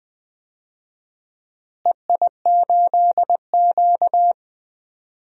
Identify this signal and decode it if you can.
Morse code, heard as a keyed tone, about 20 wpm, 700 Hz. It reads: EI8Q
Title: DXCC-100 Call Signs - 4 Characters - Repeated 2x - 20wpm